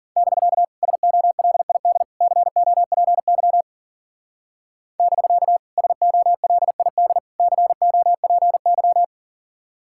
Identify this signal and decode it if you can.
Morse code, heard as a keyed tone, about 50 wpm, 700 Hz. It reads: <BK> SOLID COPY <BK> SOLID COPY